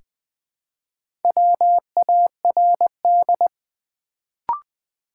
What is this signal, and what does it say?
Morse code, heard as a keyed tone, about 20 wpm, 700 Hz. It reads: WARD
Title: Build A Word - 2 Letters At A Time - 4-Letter Words - 20wpm